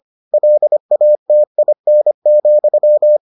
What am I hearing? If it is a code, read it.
Morse code, heard as a keyed tone, about 25 wpm, 600 Hz. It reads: LATIN,